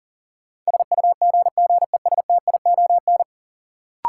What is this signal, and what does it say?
Morse code, heard as a keyed tone, about 40 wpm, 700 Hz. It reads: SUGGESTION
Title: Sentences from Top 1600 Words - New Word Drill Alt - 40wpm